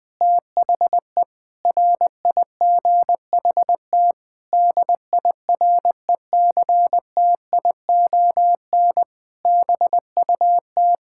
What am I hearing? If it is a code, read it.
Morse code, heard as a keyed tone, about 21 wpm, 700 Hz. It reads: THE RIGHT DIRECTION BUT
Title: Sentences with 8-Word Phrases - 20wpm